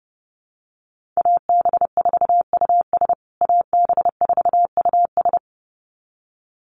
Morse code, heard as a keyed tone, about 30 wpm, 700 Hz. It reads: AB4US AB4US
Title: USA Call Signs Repeated 2x 30wpm